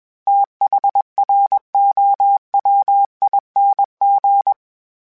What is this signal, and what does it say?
Morse code, heard as a keyed tone, about 21 wpm, 800 Hz. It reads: THROWING
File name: TH / TH mp3s / TH Prefixes 3x2 @20